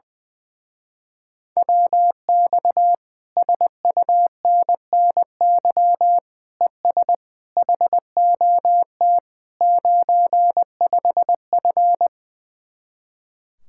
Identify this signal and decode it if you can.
Morse code, heard as a keyed tone, about 21 wpm, 700 Hz. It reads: WX SUNNY ES HOT 95F
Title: Speed-Racing Common QSO Phrases Encore - 20wpm